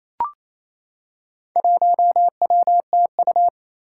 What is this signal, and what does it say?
Morse code, heard as a keyed tone, about 29 wpm, 700 Hz. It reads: 1WTU
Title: Four Challenging Letter-Number Combinations 28wpm